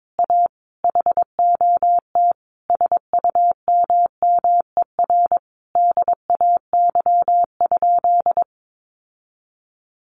Morse code, heard as a keyed tone, about 22 wpm, 700 Hz. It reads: A HOT SUMMER DAY?
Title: Sentences from Top 1500 Words 22wpm